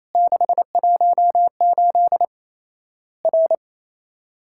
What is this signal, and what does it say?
Morse code, heard as a keyed tone, about 28 wpm, 700 Hz. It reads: 618 R
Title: Worked All Europe DX Contest - QTC Exchange - 28wpm